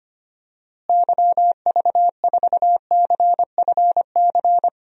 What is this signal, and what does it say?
Morse code, heard as a keyed tone, about 25 wpm, 700 Hz. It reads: YV4CFC
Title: DXCC-100 Call Signs - 6 Characters - Repeated 2x - 25wpm